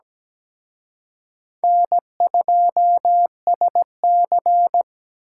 Morse code, heard as a keyed tone, about 20 wpm, 700 Hz. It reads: N2SC
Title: Speed Racing - USA 1x2 Call Signs 17wpm